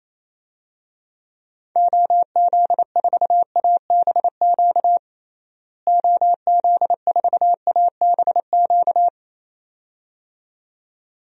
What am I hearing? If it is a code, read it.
Morse code, heard as a keyed tone, about 27 wpm, 700 Hz. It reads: OZ4ABQ OZ4ABQ
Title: DXCC-100 Call Signs - 6 Characters - Repeated 2x - 28wpm